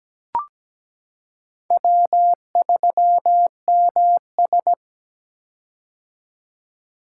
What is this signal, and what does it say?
Morse code, heard as a keyed tone, about 17 wpm, 700 Hz. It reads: W3MS
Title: K1USN SST QSO Practice - 17wpm